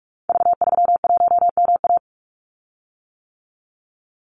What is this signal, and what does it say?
Morse code, heard as a keyed tone, about 46 wpm, 700 Hz. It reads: V31MA
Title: Call Signs - 45wpm